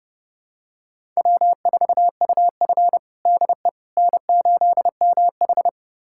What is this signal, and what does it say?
Morse code, heard as a keyed tone, about 30 wpm, 700 Hz. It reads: W4UF DE N8MH
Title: USA Callsign DE Callsign - 4 Characters - 30wpm